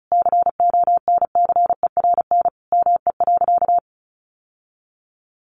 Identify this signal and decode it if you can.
Morse code, heard as a keyed tone, about 34 wpm, 700 Hz. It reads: CONCERN ME.
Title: Sentences from Top 1000 Words 35wpm